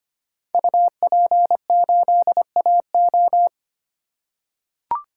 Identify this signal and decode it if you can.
Morse code, heard as a keyed tone, about 25 wpm, 700 Hz. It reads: UP8AO
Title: Code Groups - 5 Letter:Numbers - 25wpm